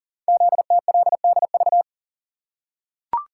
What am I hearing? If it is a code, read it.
Morse code, heard as a keyed tone, about 38 wpm, 700 Hz. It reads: ZTLDV
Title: Code Groups - 5 Letters - 40wpm